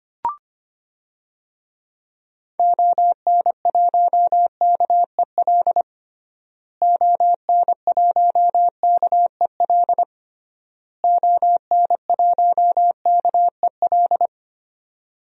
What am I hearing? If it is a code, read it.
Morse code, heard as a keyed tone, about 25 wpm, 700 Hz. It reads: ON1KEL ON1KEL ON1KEL
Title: DXCC-100 Call Signs - 6 Characters - Repeated 3x - 25wpm